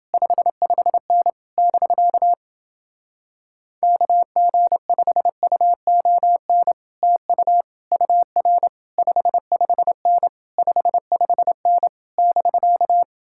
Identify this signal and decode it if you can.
Morse code, heard as a keyed tone, about 29 wpm, 700 Hz. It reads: 55N <BK> KG5UON TU UR 55N 55N <BK>